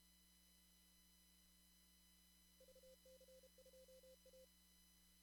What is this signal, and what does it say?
Morse code, heard as a keyed tone, about 31 wpm, 520 Hz. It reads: VC2A
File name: CW / kt5j wpx cw-op-k5pi